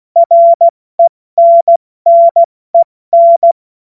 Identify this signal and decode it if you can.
Morse code, heard as a keyed tone, about 16 wpm, spaced as for 14 wpm, 680 Hz. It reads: RENNEN
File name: Morsekurs / Witze / Witz 07. Dezember 2021, 70 F80